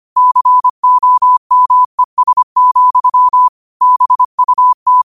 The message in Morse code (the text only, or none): COMES, BUT